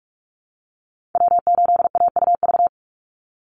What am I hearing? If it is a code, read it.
Morse code, heard as a keyed tone, about 44 wpm, 700 Hz. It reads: W8AUV
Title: USA Call Signs 45wpm